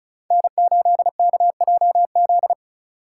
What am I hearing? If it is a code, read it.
Morse code, heard as a keyed tone, about 36 wpm, 700 Hz. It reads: N8KJZ
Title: Code Groups - 5 Letter:Numbers - 35wpm